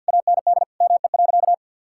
Morse code, heard as a keyed tone, about 50 wpm, 700 Hz. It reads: AND ME.
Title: Sentences from Top 1600 Words 50wpm